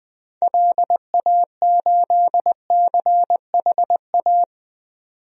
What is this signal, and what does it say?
Morse code, heard as a keyed tone, about 20 wpm, 700 Hz. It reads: LA8CHA
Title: DXCC-100 Call Signs - 6 Characters - 20wpm